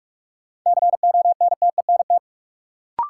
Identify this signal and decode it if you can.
Morse code, heard as a keyed tone, about 46 wpm, 700 Hz. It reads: CONTENT
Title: Sentences from Top 900 Words - New Word Drill - 45wpm